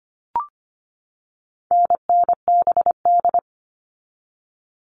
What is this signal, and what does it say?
Morse code, heard as a keyed tone, about 25 wpm, 700 Hz. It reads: NNBD
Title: All Cut Numbers - 4-Digit 25wpm